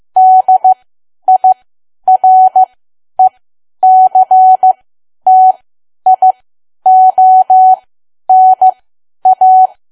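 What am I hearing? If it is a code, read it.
Morse code, heard as a keyed tone, about 15 wpm, spaced as for 10 wpm, 750 Hz. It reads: DIRECTIONA
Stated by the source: Morse learning practice at 10 wpm.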